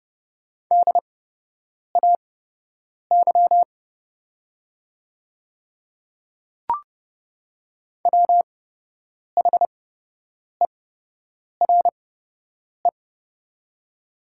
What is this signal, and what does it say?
Morse code, heard as a keyed tone, about 30 wpm, spaced as for 8 wpm, 700 Hz. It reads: DAY WHERE
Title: 8x Character Spacing - Top 200 Words - 30wpm